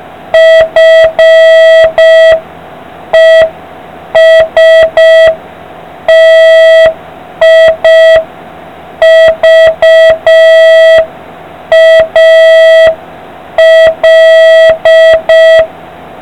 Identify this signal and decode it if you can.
Morse code, heard as a keyed tone, about 5 wpm, 1950 Hz. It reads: FESTIVAL